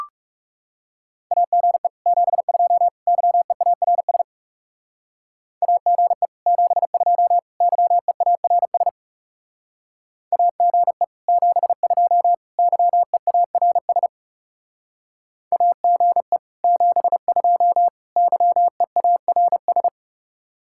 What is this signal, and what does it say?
Morse code, heard as a keyed tone, about 39 wpm, 700 Hz. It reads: AGE 72 YEARS AGE 72 YEARS AGE 72 YEARS AGE 72 YEARS